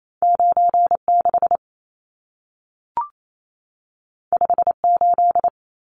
Morse code, heard as a keyed tone, about 28 wpm, 700 Hz. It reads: 96 58
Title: Numbers - 2-Digit 28wpm